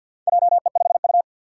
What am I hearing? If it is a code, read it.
Morse code, heard as a keyed tone, about 51 wpm, 700 Hz. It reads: JE5U
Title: Four Challenging Letter-Number Combinations 50wpm